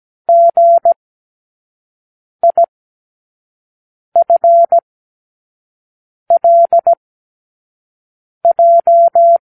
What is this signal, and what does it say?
Morse code, heard as a keyed tone, about 17 wpm, spaced as for 5 wpm, 680 Hz. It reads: GIFLJ
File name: k7qo / Lesson18(A-L)